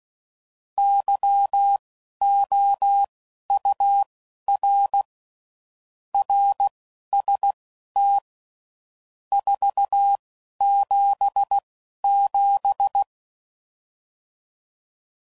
Morse code, heard as a keyed tone, about 16 wpm, spaced as for 11 wpm, 800 Hz. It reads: YOUR RST 477
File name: CW Code Groups @10WPM / RSTs / Your RST's